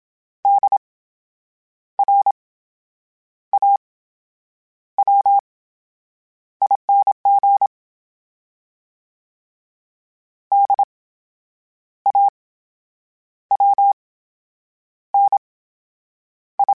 Morse code, heard as a keyed tone, about 27 wpm, spaced as for 8 wpm, 800 Hz. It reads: DRAWING DAWNI
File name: Special INGs / Special INGs-mp3s / VerySlowToFaster Ing @25